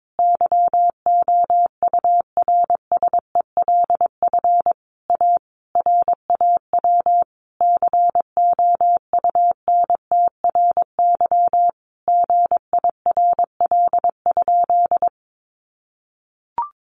Morse code, heard as a keyed tone, about 22 wpm, 700 Hz. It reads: YOURSELF A RAW COUNTRY GIRL?
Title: Sentences from Top 1500 Words 22wpm